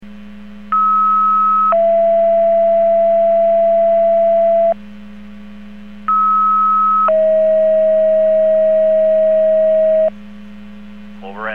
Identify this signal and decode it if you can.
Two-tone page, heard as a one-second tone, then a three-second tone, about 1290 then 670 Hz, about 1290 then 640 Hz. It is heard from 0.7 to 10.1 s.